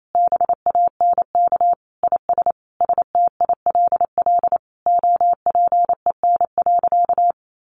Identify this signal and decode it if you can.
Morse code, heard as a keyed tone, about 28 wpm, 700 Hz. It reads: BANK IS STILL OPEN.
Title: Sentences from Top 1500 Words 28wpm